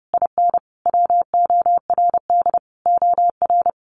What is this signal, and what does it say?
Morse code, heard as a keyed tone, about 30 wpm, 700 Hz. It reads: IN WORD OR
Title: Sentences from Top 1200 Words 30wpm